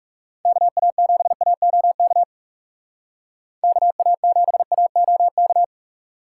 Morse code, heard as a keyed tone, about 37 wpm, 700 Hz. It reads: KA7AOK KA7AOK
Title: Speed Racing - USA Call Signs 30wpm